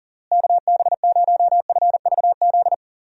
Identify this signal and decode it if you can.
Morse code, heard as a keyed tone, about 40 wpm, 700 Hz. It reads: KB0FVZ
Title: USA Call Signs - 6 Characters - Repeated 3x - 40wpm